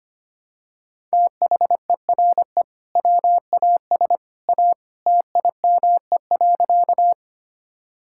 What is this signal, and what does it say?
Morse code, heard as a keyed tone, about 25 wpm, 700 Hz. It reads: THERE WAS A TIME.
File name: practice / 25wpm / Sentences with 2-Word Phrases - 25wpm